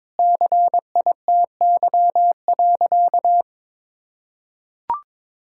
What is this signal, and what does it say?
Morse code, heard as a keyed tone, about 22 wpm, 700 Hz. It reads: CITY.
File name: practice / 22wpm / Sentences with 4-Word Phrases - 22wpm